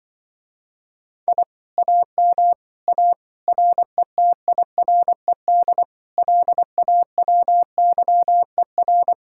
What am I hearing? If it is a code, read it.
Morse code, heard as a keyed tone, about 24 wpm, 700 Hz. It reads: I AM A RETIRED LAWYER
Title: Speed-Racing Common QSO Phrases - 20wpm